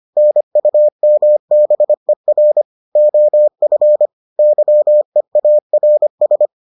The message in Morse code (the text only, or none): NUMBER OF YEARS